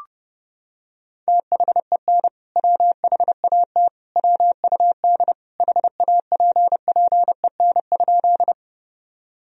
Morse code, heard as a keyed tone, about 30 wpm, 700 Hz. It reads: THEN WHAT WUD HAPPEN?
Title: Sentences from Top 400 Words and Abbreviations 30wpm